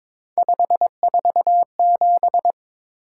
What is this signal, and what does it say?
Morse code, heard as a keyed tone, about 22 wpm, 700 Hz. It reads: 547
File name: practice / 22wpm / Three Letter-Number Combinations - 22wpm